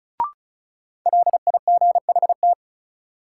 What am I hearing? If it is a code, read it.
Morse code, heard as a keyed tone, about 36 wpm, 700 Hz. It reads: LIGHT
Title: Sentences from Top 200 Words - New Word Drill - 35wpm